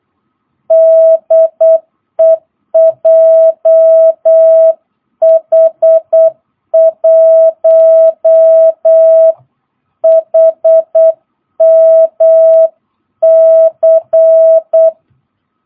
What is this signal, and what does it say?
Morse code, heard as a keyed tone, about 8 wpm, 650 Hz. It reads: DE JH1HMC